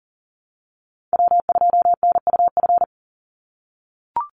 Morse code, heard as a keyed tone, about 40 wpm, 700 Hz. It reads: W2NUF
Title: USA Call Signs 40wpm